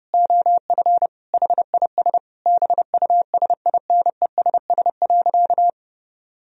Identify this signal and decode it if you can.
Morse code, heard as a keyed tone, about 29 wpm, 700 Hz. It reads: OF HIS BUSINESS.
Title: Sentences from Top 1000 Words - Review Part 3 - 30wpm